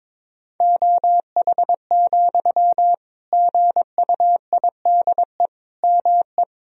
Morse code, heard as a keyed tone, about 21 wpm, 700 Hz. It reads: OH, GUIDE ME
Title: Sentences from Top 1000 Words - Review Part 3 - 22wpm